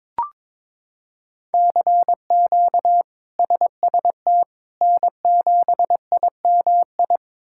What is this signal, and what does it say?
Morse code, heard as a keyed tone, about 21 wpm, 700 Hz. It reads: CQ SST N7IMI
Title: K1USN SST QSO Practice - 22wpm